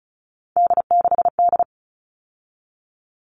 Morse code, heard as a keyed tone, about 34 wpm, 700 Hz. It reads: D6D
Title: All Cut Numbers - 3-Digit 35wpm